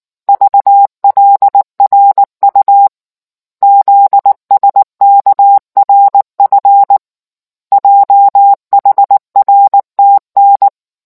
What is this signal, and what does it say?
Morse code, heard as a keyed tone, about 19 wpm, 800 Hz. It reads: VLRU ZSKRF JHRTN